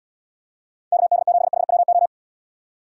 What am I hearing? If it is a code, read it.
Morse code, heard as a keyed tone, about 73 wpm, 700 Hz. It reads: KD7HWO